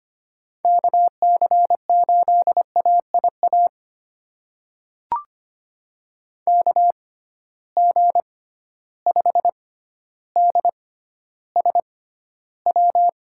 Morse code, heard as a keyed tone, about 25 wpm, spaced as for 13 wpm, 700 Hz. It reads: KC8AIA KG5DSW
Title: USA Call Signs - 6 Characters - 6x Character Spacing - 25wpm